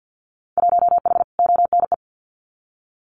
Morse code, heard as a keyed tone, about 50 wpm, 700 Hz. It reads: 1H ONE
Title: Winter Field Day Exchanges 50wpm